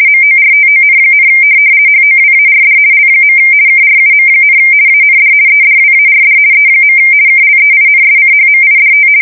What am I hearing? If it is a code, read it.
RTTY, about 50 baud. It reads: MORTON, HOWARD L. KRUM, AND EDWARD E. KLEINSCHMIDT
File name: fsk170-50B / teletype-corp